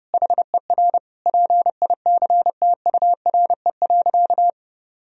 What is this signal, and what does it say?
Morse code, heard as a keyed tone, about 32 wpm, 700 Hz. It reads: HER PICTURE.
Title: Sentences from Top 1000 Words - Review Part 1 - 30wpm